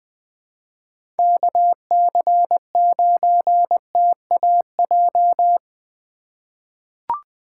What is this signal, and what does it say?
Morse code, heard as a keyed tone, about 20 wpm, 700 Hz. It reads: KC9TAJ
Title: USA Call Signs - 6 Characters - 8x Character Spacing - 20wpm